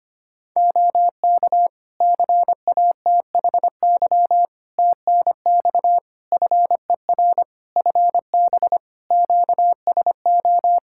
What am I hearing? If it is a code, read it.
Morse code, heard as a keyed tone, about 25 wpm, 700 Hz. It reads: OK CATHY TNX FER FB QSO
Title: Practice QSO - Exchange 4 - Ending - 25wpm